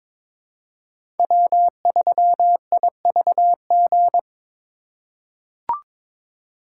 Morse code, heard as a keyed tone, about 22 wpm, 700 Hz. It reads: W3IVG
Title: USA Call Signs - 5 Characters - 6x Character Spacing - 22wpm